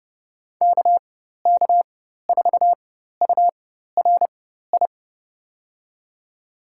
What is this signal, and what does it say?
Morse code, heard as a keyed tone, about 30 wpm, spaced as for 14 wpm, 700 Hz. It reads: KK4URI
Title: USA Call Signs - 4x Character Spacing - 30wpm